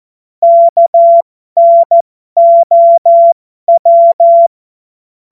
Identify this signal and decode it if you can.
Morse code, heard as a keyed tone, about 14 wpm, spaced as for 12 wpm, 680 Hz. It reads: KNOW
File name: Morsekurs / Witze